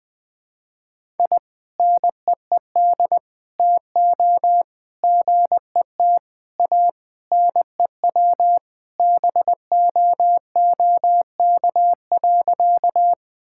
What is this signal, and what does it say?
Morse code, heard as a keyed tone, about 19 wpm, 700 Hz. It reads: I NEED TO GET A NEW BOOK.
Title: Sentences with 3-Word Phrases - Spoken Beginning - 20wpm